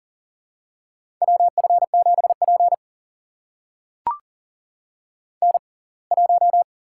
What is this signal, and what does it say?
Morse code, heard as a keyed tone, about 40 wpm, spaced as for 27 wpm, 700 Hz. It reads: WF7P N1
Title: USA Call Signs - 6x Character Spacing - 40wpm